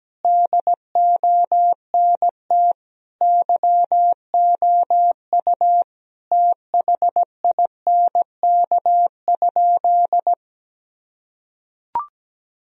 Morse code, heard as a keyed tone, about 17 wpm, 700 Hz. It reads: DONT YOU THINK?
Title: Sentences from Top 1000 Words - Review Part 4 - 17wpm